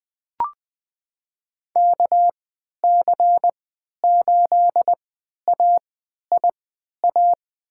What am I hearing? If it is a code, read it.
Morse code, heard as a keyed tone, about 20 wpm, spaced as for 11 wpm, 700 Hz. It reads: KC8AIA
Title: USA Call Signs - 6 Characters - 3x Character Spacing - 20wpm